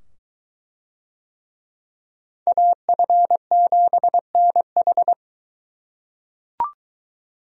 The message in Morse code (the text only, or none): AF7NH